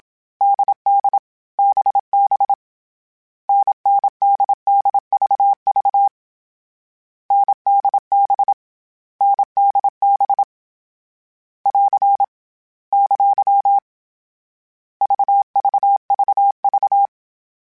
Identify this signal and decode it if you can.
Morse code, heard as a keyed tone, about 27 wpm, 800 Hz. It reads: DD BB NNDDVV NDB NDB <AR> ! VVVV
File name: Variable Speed / Variable Speed-mp3s / GZ GZ7 sp @25